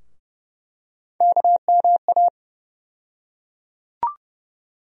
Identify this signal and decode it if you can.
Morse code, heard as a keyed tone, about 30 wpm, 700 Hz. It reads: KMA
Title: USA Call Sign Postfixes - 3 Letters 30wpm